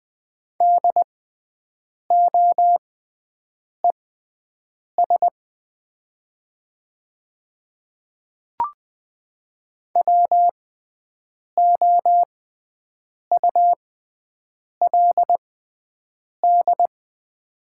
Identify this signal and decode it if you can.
Morse code, heard as a keyed tone, about 20 wpm, spaced as for 7 wpm, 700 Hz. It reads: DOES WOULD
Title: 6x Character Spacing - Top 300 Words - 20wpm